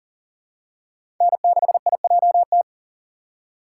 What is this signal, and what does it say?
Morse code, heard as a keyed tone, about 40 wpm, 700 Hz. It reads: N6IJT
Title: USA Call Signs - 5 Characters - 2x Character Spacing - 40wpm